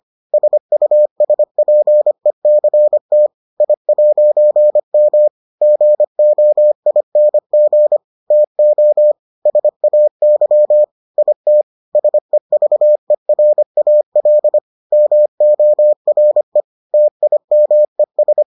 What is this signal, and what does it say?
Morse code, heard as a keyed tone, about 25 wpm, 600 Hz. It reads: SUSPECT I'M GOING TO SAY IT SEVERAL MORE TIMES